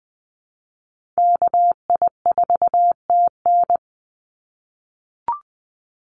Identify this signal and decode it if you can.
Morse code, heard as a keyed tone, about 20 wpm, 700 Hz. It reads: KI4TN
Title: NA SOTA Chaser Callsigns 20wpm